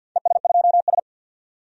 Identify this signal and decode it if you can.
Morse code, heard as a keyed tone, about 53 wpm, 700 Hz. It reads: ES2S
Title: DXCC-100 Call Signs - 4 Characters - 50wpm